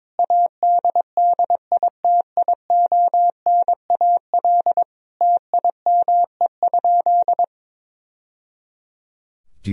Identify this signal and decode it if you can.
Morse code, heard as a keyed tone, about 21 wpm, 700 Hz. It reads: ADDITIONAL TIME?
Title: Sentences from Top 1000 Words - Review Part 4 - 22wpm